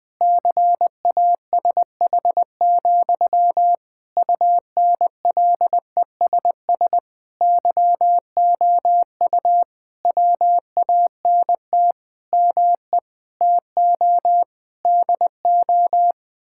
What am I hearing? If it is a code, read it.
Morse code, heard as a keyed tone, about 20 wpm, 700 Hz. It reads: CASH, UNLESS YOU WANT ME TO DO
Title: Sentences from Top 1000 Words - Review Part 6 - 20wpm